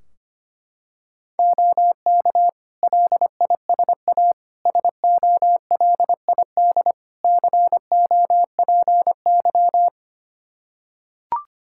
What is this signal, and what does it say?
Morse code, heard as a keyed tone, about 25 wpm, 700 Hz. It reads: OK LISA SOLID COPY
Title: Practice QSO - Exchange 3 - More Info - 25wpm